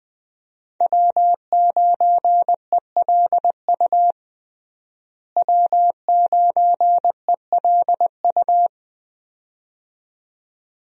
Morse code, heard as a keyed tone, about 20 wpm, 700 Hz. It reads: W9ELU W9ELU
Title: USA Call Signs - 5 Characters - Repeated 2x - 20wpm